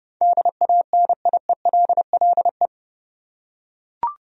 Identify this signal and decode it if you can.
Morse code, heard as a keyed tone, about 32 wpm, 700 Hz. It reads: DANIELLE
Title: USA Common Names - 30wpm